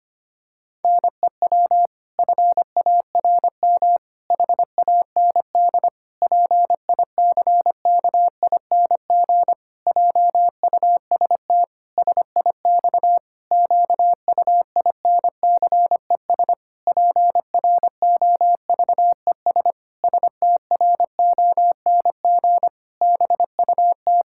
Morse code, heard as a keyed tone, about 25 wpm, 700 Hz. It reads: NEW FARM HAND PICKING JUST SIX QUINCES PROVES STRONG BUT
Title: Pangram Challenge v2 - 25wpm